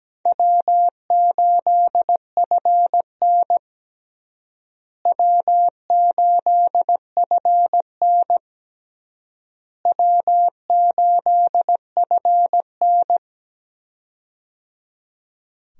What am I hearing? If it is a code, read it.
Morse code, heard as a keyed tone, about 17 wpm, 700 Hz. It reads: W8FN W8FN W8FN
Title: USA Call Signs - 4 Characters - Repeated 3x - 17wpm